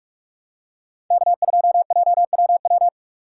Morse code, heard as a keyed tone, about 46 wpm, 700 Hz. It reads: K2JWW
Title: USA Call Signs - 5 Characters - 2x Character Spacing - 45wpm